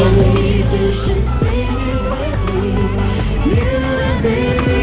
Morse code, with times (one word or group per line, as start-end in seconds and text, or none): none